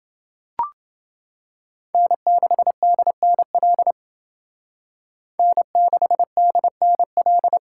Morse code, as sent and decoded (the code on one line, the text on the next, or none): -. -.... -.. -. .-.. / -. -.... -.. -. .-..
N6DNL N6DNL